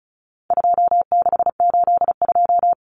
Morse code, heard as a keyed tone, about 34 wpm, 700 Hz. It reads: 2682
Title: Common Cut Numbers - 4-Digit 35wpm